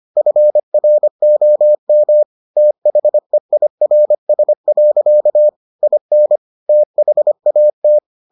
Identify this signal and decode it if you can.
Morse code, heard as a keyed tone, about 25 wpm, 600 Hz. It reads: FROM THEIRS. IN THAT